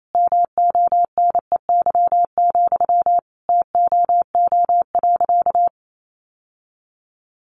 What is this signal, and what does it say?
Morse code, heard as a keyed tone, about 28 wpm, 700 Hz. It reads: MONEY, TOO.
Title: Sentences from Top 900 Words 28wpm